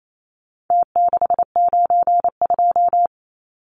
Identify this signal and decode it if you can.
Morse code, heard as a keyed tone, about 28 wpm, 700 Hz. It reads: T692
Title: Numbers - Contest Style - 4-Digit 28wpm